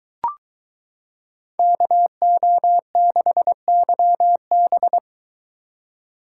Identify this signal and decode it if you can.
Morse code, heard as a keyed tone, about 23 wpm, 700 Hz. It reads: KO6YB